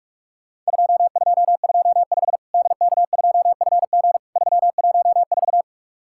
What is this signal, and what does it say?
Morse code, heard as a keyed tone, about 44 wpm, 700 Hz. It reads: 2225 DK2FG 314
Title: Worked All Europe DX Contest - QTC Exchange - 45wpm